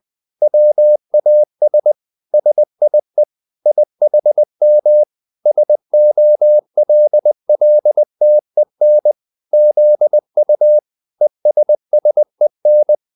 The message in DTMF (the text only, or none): none